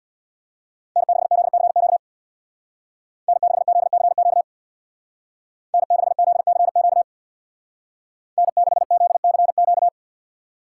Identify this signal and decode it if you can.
Morse code, heard as a keyed tone, about 62 wpm, 700 Hz. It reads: N6ZCQ N6ZCQ N6ZCQ N6ZCQ
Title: Speed Racing - USA Call Signs 50wpm